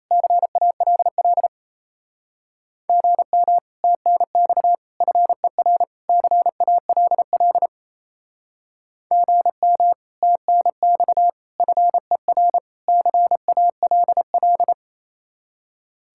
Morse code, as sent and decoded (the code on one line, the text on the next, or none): -.-. .- .-.. .-.. / --. -- / - -. -..- / ..-. .-. / -.-. .- .-.. .-.. / --. -- / - -. -..- / ..-. .-. / -.-. .- .-.. .-..
CALL GM TNX FER CALL GM TNX FER CALL